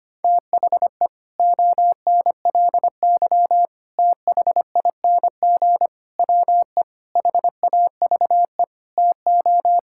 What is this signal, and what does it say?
Morse code, heard as a keyed tone, about 25 wpm, 700 Hz. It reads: THE ONLY THING WE HAVE TO
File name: Sets of 9 Words - 25wpm